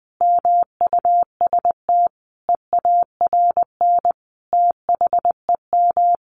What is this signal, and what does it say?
Morse code, heard as a keyed tone, about 20 wpm, 700 Hz. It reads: MUST EARN THEM
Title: Sentences from Top 1500 Words 20wpm